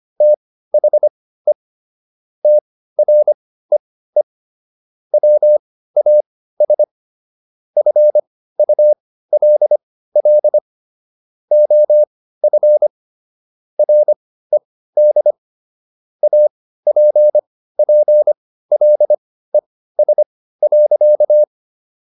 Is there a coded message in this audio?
Yes: Morse code from 0.2 to 21.4 s, heard as a keyed tone, about 25 wpm, spaced as for 15 wpm, 600 Hz. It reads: THE TREE WAS FULL OF RED APPLES.